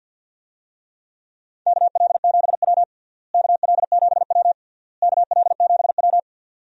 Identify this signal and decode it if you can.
Morse code, heard as a keyed tone, about 53 wpm, 700 Hz. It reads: KL7W KL7W KL7W
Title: DXCC-100 Call Signs - 4 Characters - Repeated 3x - 50wpm